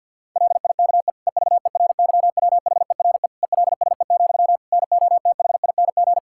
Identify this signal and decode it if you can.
Morse code, heard as a keyed tone, about 52 wpm, 700 Hz. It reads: LIKE EVERYWHERE ELSE, NOTHING
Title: Sentences from Top 1600 Words 50wpm